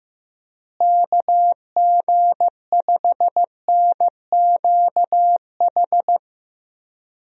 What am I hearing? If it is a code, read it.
Morse code, heard as a keyed tone, about 15 wpm, 700 Hz. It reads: KG5NQH
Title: USA Call Signs - 6 Characters - Repeated 2x - 15wpm